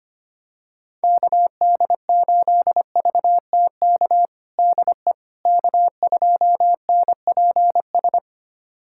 Morse code, heard as a keyed tone, about 25 wpm, 700 Hz. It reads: KD8VTK DE K2NPS
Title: USA Callsign DE Callsign 25wpm